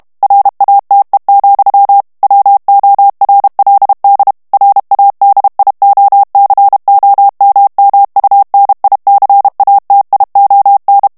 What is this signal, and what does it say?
Morse code, heard as a keyed tone, about 32 wpm, 800 Hz. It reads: RATE, WORLD RADIOCOMMUNICATION
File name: ANGLAIS texte clair 3 vitesse 30 mots minute